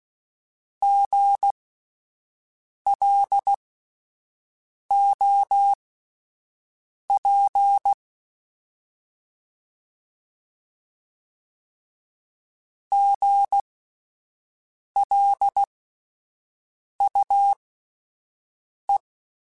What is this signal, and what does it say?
Morse code, heard as a keyed tone, about 16 wpm, spaced as for 5 wpm, 800 Hz. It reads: GLOP GLUE
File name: Spaced Alphas / Spaced Alphas mp3s / Gs sp @10